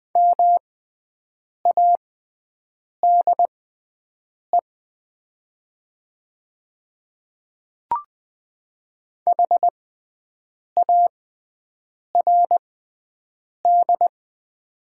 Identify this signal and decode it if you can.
Morse code, heard as a keyed tone, about 20 wpm, spaced as for 7 wpm, 700 Hz. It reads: MADE HARD
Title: 6x Character Spacing - Top 500 Words - 20wpm